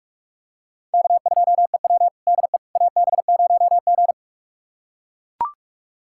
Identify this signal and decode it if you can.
Morse code, heard as a keyed tone, about 42 wpm, 700 Hz. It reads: K2EW DE AB0G
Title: USA Callsign DE Callsign - 4 Characters - 45wpm